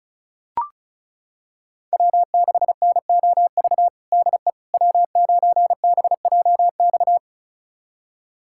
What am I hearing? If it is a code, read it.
Morse code, heard as a keyed tone, about 36 wpm, 700 Hz. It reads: W6NOV DE W9BJX
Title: USA Callsign DE Callsign - 5 Characters - 35wpm